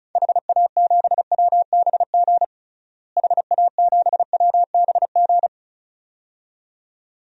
Morse code, heard as a keyed tone, about 36 wpm, 700 Hz. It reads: HA7WBG HA7WBG
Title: DXCC-100 Call Signs - 6 Characters - Repeated 2x - 35wpm